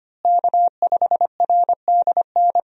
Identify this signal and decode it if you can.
Morse code, heard as a keyed tone, about 25 wpm, 700 Hz. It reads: K5RDN
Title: USA Call Signs - 5 Characters - Repeated 2x - 25wpm